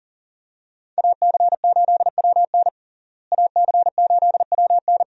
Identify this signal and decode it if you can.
Morse code, heard as a keyed tone, about 40 wpm, 700 Hz. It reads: AC8WN AC8WN
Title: USA Call Signs - 5 Characters - Repeated 2x - 40wpm